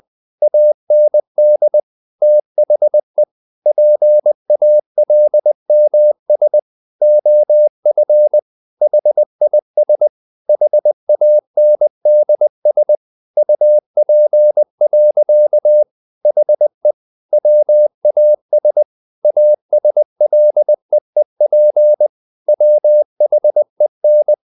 Morse code, heard as a keyed tone, about 20 wpm, 600 Hz. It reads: AND THE PALMS OF HIS HANDS UP. HE WAS ASLEEP WHEN
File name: the old man and the sea part 2 20wpm